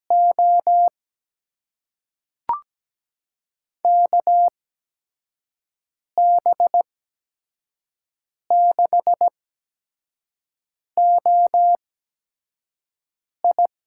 Morse code, heard as a keyed tone, about 17 wpm, spaced as for 5 wpm, 700 Hz. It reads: O KB6OI
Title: USA Call Signs - 6 Characters - 8x Character Spacing - 17wpm